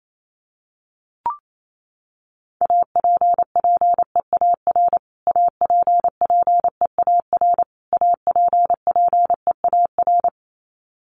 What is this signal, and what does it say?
Morse code, heard as a keyed tone, about 28 wpm, 700 Hz. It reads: APPEAR APPEAR APPEAR
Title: Top 500 Words Repeated 28wpm